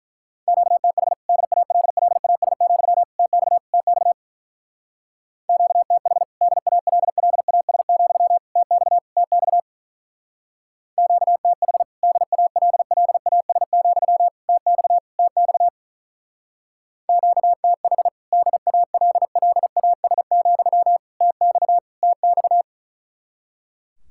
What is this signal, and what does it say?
Morse code, heard as a keyed tone, about 43 wpm, 700 Hz. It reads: QTH DALLAS, TX TX QTH DALLAS, TX TX QTH DALLAS, TX TX QTH DALLAS, TX TX
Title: Speed-Racing Common QSO Phrases - 35wpm